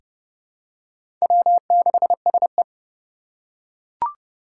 Morse code, heard as a keyed tone, about 30 wpm, 700 Hz. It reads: W6SE
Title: USA Call Signs - 4 Characters - 2x Character Spacing - 30wpm